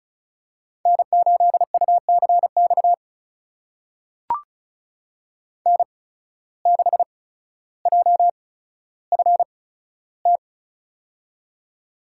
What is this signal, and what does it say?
Morse code, heard as a keyed tone, about 36 wpm, spaced as for 15 wpm, 700 Hz. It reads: N8UCX N6JFT
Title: USA Call Signs - 5 Characters - 8x Character Spacing - 35wpm